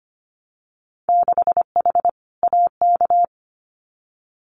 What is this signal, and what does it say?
Morse code, heard as a keyed tone, about 25 wpm, 690 Hz. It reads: 6H AK
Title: Winter Field Day Exchanges Encore 25wpm